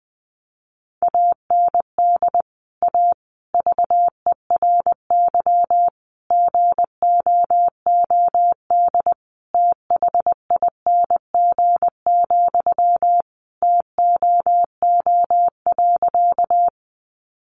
Morse code, heard as a keyed tone, about 20 wpm, 700 Hz. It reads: AND A VERY GOOD THING, TOO.
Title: Sentences from Top 200 Words 20wpm